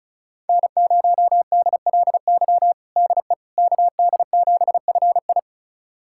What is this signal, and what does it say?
Morse code, heard as a keyed tone, about 36 wpm, 700 Hz. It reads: N0DLY DE KD7FI
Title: USA Callsign DE Callsign - 5 Characters - 35wpm